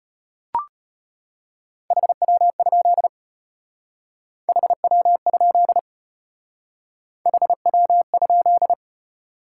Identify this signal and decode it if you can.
Morse code, heard as a keyed tone, about 33 wpm, 700 Hz. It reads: HW? HW? HW?